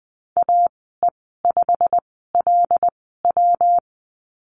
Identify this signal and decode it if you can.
Morse code, heard as a keyed tone, about 20 wpm, spaced as for 14 wpm, 700 Hz. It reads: AE5LW
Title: 154 Lesson 25g - Letter p - Call Signs 20wpm